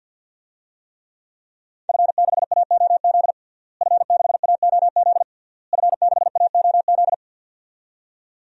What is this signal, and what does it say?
Morse code, heard as a keyed tone, about 52 wpm, 700 Hz. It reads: F6AOZ F6AOZ F6AOZ
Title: DXCC-100 Call Signs - 5 Characters - Repeated 3x - 50wpm